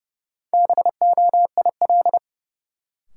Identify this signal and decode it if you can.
Morse code, heard as a keyed tone, about 30 wpm, 700 Hz. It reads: BOIL
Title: Sets of 5 Words Encore - New Word Drill - 30wpm